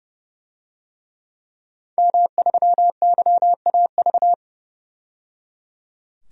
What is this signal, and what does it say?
Morse code, heard as a keyed tone, about 30 wpm, 700 Hz. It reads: M3YAV
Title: DXCC-100 Call Signs - 5 Characters - 30wpm